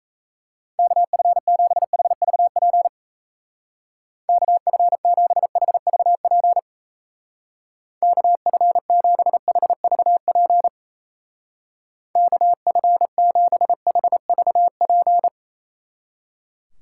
Morse code, heard as a keyed tone, about 35 wpm, 700 Hz. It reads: KF7HVP KF7HVP KF7HVP KF7HVP
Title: Speed Racing - USA 2x3 Call Signs 28wpm